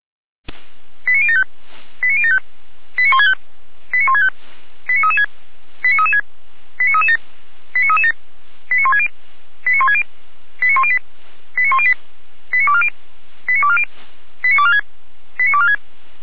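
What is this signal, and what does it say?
Five-tone selective call (ZVEI), heard as quick runs of five tones, tones of about 70 ms. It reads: 89075 89075 89165 89165 89336 89336 89337 89337 89160 89160 89108 89108 89240 89240 89246 89246
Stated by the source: ZVEI five-tone call 89167 received on SDR, served by feuerwehr-orlamuende.org (fire brigade Orlamünde).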